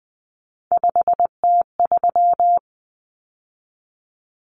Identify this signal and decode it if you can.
Morse code, heard as a keyed tone, about 20 wpm, 700 Hz. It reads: 5T3
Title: Common Cut Numbers - 3-Digit 20wpm